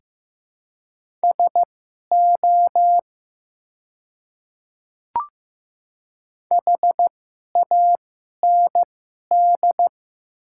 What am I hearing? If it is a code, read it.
Morse code, heard as a keyed tone, about 15 wpm, spaced as for 11 wpm, 700 Hz. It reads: SO HAND
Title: 2x Character Spacing - Top 200 Words - 15wpm